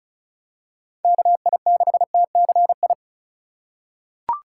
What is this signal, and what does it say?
Morse code, heard as a keyed tone, about 36 wpm, 700 Hz. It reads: KI6TCI
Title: USA Call Signs - 6 Characters - 3x Character Spacing - 35wpm